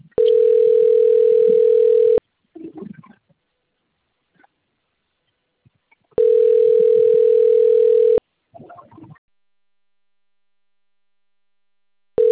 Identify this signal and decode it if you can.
Ringback tone (North America), 440 + 480 Hz.
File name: rg-1600-unknown- - - .84009